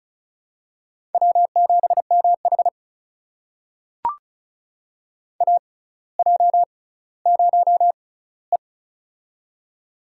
Morse code, heard as a keyed tone, about 36 wpm, spaced as for 18 wpm, 700 Hz. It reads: W7MH AJ0E
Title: USA Call Signs - 4 Characters - 6x Character Spacing - 35wpm